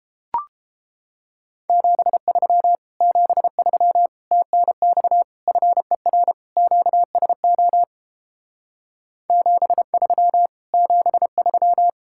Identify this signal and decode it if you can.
Morse code, heard as a keyed tone, about 29 wpm, 700 Hz. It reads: 73 73 TNX FER QSO 73 73